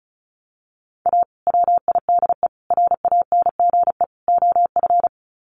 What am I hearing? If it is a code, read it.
Morse code, heard as a keyed tone, about 34 wpm, 700 Hz. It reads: A WIDE RANGE OF
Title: Sets of 4 Words 35wpm